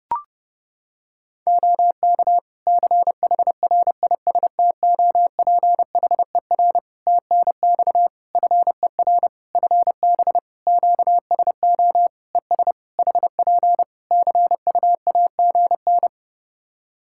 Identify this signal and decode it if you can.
Morse code, heard as a keyed tone, about 32 wpm, 700 Hz. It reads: OK CHRISTOPHER TNX FER FB QSO ES HP CUAGN
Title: Practice QSO - Exchange 4 - Ending - 30wpm